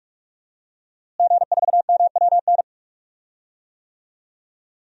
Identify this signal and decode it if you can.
Morse code, heard as a keyed tone, about 44 wpm, 700 Hz. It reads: G4MWN
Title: DXCC-100 Call Signs - 5 Characters - Repeated 3x - 45wpm